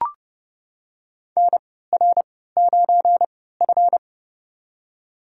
Morse code, heard as a keyed tone, about 31 wpm, spaced as for 17 wpm, 700 Hz. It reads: NR9F
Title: USA Call Signs - 4 Characters - 3x Character Spacing - 30wpm